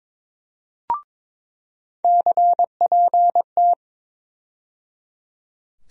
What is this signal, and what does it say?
Morse code, heard as a keyed tone, about 22 wpm, 700 Hz. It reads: CPT
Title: USA Call Sign Postfixes - 3 Letters 22wpm